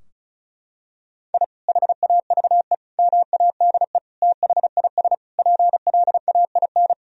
Morse code, heard as a keyed tone, about 36 wpm, 700 Hz. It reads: I HAVE MADE THIS PLAIN
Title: Sentences from Top 1000 Words - Review Part 2 - 35wpm